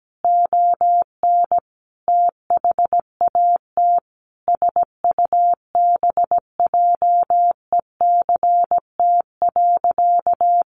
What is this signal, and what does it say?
Morse code, heard as a keyed tone, about 17 wpm, 700 Hz. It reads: ON THAT SUBJECT.
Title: Sentences from Top 800 Words 17wpm